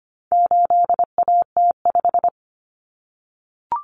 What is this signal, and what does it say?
Morse code, heard as a keyed tone, about 25 wpm, 700 Hz. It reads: 8AT5
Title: Common Cut Numbers - 4-Digit 25wpm